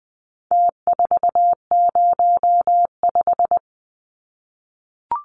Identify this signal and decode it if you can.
Morse code, heard as a keyed tone, about 20 wpm, 700 Hz. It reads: T405
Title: Numbers - Contest Style - 4-Digit 20wpm